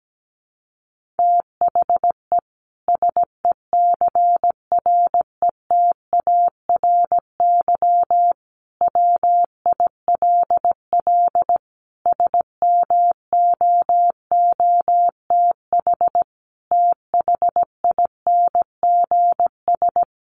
Morse code, heard as a keyed tone, about 17 wpm, 700 Hz. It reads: THE SECRETARY WILL SMOOTH THINGS
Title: Sentences from Top 1400 Words 17wpm